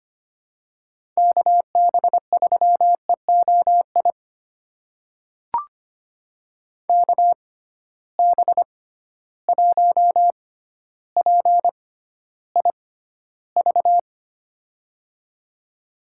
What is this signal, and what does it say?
Morse code, heard as a keyed tone, about 25 wpm, spaced as for 13 wpm, 700 Hz. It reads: KB3EOI KB1PIV
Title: USA Call Signs - 6 Characters - 6x Character Spacing - 25wpm